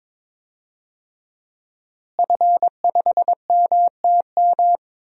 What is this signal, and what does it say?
Morse code, heard as a keyed tone, about 22 wpm, 700 Hz. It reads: F5MTM
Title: DXCC-100 Call Signs - 5 Characters - Repeated 3x - 22wpm